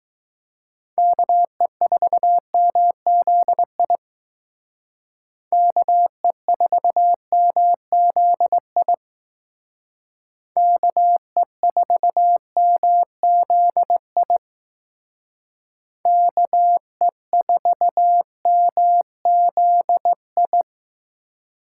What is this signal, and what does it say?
Morse code, heard as a keyed tone, about 19 wpm, 700 Hz. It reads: KE4MZI KE4MZI KE4MZI KE4MZI
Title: Speed Racing - USA 2x3 Call Signs 15wpm